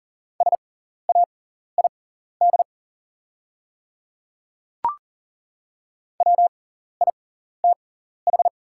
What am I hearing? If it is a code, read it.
Morse code, heard as a keyed tone, about 40 wpm, spaced as for 14 wpm, 700 Hz. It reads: SAID WITH
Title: 6x Character Spacing - Top 100 Words - 40wpm